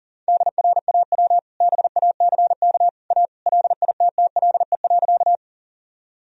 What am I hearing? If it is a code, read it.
Morse code, heard as a keyed tone, about 40 wpm, 700 Hz. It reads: DRAW BACK A LITTLE.
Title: Sentences from Top 1000 Words - Review Part 1 - 40wpm